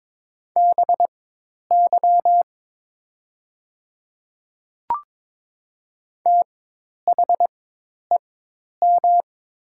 Morse code, heard as a keyed tone, about 22 wpm, spaced as for 10 wpm, 700 Hz. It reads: BY THEM
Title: 4x Character Spacing - Top 200 Words - 22wpm